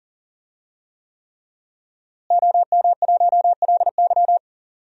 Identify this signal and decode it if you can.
Morse code, heard as a keyed tone, about 40 wpm, 700 Hz. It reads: OM1LY
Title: DXCC-100 Call Signs - 5 Characters - Repeated 3x - 40wpm